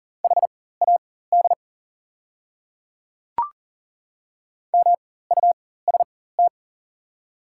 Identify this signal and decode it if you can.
Morse code, heard as a keyed tone, about 40 wpm, spaced as for 19 wpm, 700 Hz. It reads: HAD MUST
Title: 4x Character Spacing - Top 500 Words - 40wpm